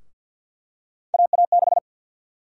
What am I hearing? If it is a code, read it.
Morse code, heard as a keyed tone, about 50 wpm, 700 Hz. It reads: AA6
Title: USA Call Sign Prefixes - 50wpm